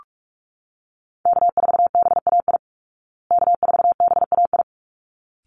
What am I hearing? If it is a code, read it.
Morse code, heard as a keyed tone, about 44 wpm, 700 Hz. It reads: K4BAI K4BAI
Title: Call Signs Repeated 3x - 45wpm